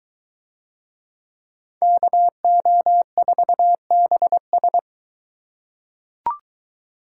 Morse code, heard as a keyed tone, about 21 wpm, 700 Hz. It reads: KO4BS